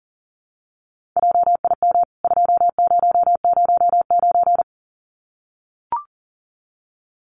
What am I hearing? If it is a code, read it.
Morse code, heard as a keyed tone, about 40 wpm, 700 Hz. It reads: JIM 2009
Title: CWT Exchange 40wpm